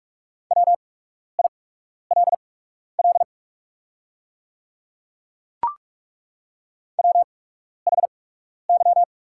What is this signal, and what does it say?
Morse code, heard as a keyed tone, about 43 wpm, spaced as for 12 wpm, 700 Hz. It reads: WILL WHY
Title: 8x Character Spacing - Top 300 Words - 45wpm